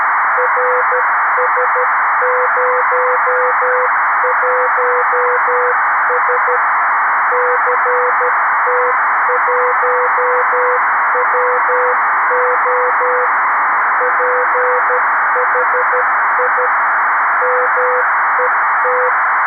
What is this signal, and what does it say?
Morse code, heard as a keyed tone, about 13 wpm, 500 Hz, with noise. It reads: RS01S CT1WO PHI MET
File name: arissat / recordings / ARISSat 11Sept2011 1354utc 60frames